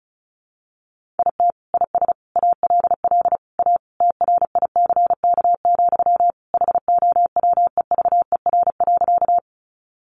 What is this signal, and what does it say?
Morse code, heard as a keyed tone, about 34 wpm, 700 Hz. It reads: IT IS ALL A TRICK, HOWEVER.
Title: Sentences from Top 1500 Words 35wpm